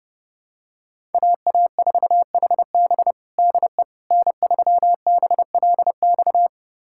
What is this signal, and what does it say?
Morse code, heard as a keyed tone, about 30 wpm, 700 Hz. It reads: AA4HB DE N3BLX
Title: USA Callsign DE Callsign - 5 Characters - 30wpm